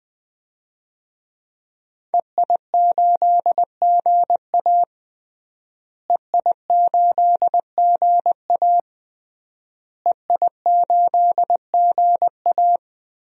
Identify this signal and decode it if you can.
Morse code, heard as a keyed tone, about 20 wpm, 700 Hz. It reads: EI8GA EI8GA EI8GA
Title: DXCC-100 Call Signs - 5 Characters - Repeated 3x - 20wpm